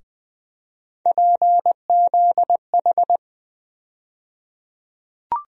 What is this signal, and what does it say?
Morse code, heard as a keyed tone, about 20 wpm, 700 Hz. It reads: PZH
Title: USA Call Sign Postfixes - 3 Letters 20wpm